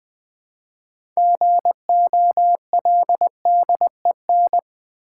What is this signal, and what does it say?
Morse code, heard as a keyed tone, about 20 wpm, 700 Hz. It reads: GOLDEN
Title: 6 Letter Words - First 3 Letters Given - 20wpm